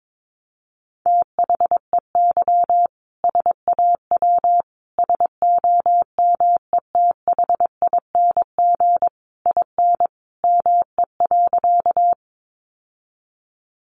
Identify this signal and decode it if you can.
Morse code, heard as a keyed tone, about 22 wpm, 700 Hz. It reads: THEY SAW SOMETHING IN ME.
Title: Sentences from Top 600 Words 22wpm